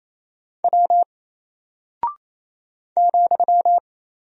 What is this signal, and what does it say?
Morse code, heard as a keyed tone, about 28 wpm, 700 Hz. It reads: W ,
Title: Single Character (Letter:Number:Punctuation:Prosign) Alt 28wpm